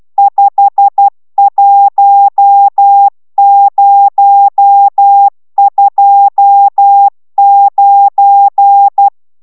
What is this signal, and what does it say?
Morse code, heard as a keyed tone, about 12 wpm, 800 Hz. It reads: 51029